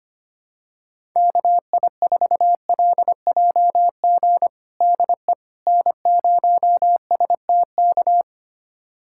Morse code, heard as a keyed tone, about 25 wpm, 700 Hz. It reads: KI4LJG DE N0STK
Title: USA Callsign DE Callsign 25wpm